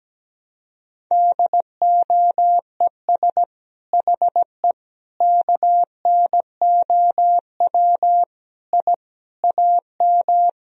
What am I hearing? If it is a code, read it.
Morse code, heard as a keyed tone, about 17 wpm, 700 Hz. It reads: DOES HE KNOW I AM
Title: Sentences from Top 1000 Words - Review Part 2 - 17wpm